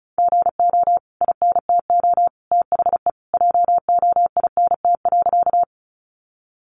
Morse code, heard as a keyed tone, about 34 wpm, 700 Hz. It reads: GO INTO THE JOINT.
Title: Sentences from Top 1300 Words 35wpm